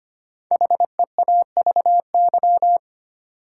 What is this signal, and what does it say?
Morse code, heard as a keyed tone, about 25 wpm, 700 Hz. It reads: HEAVY
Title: Sentences from Top 1100 Words - New Word Drill Alt - 25wpm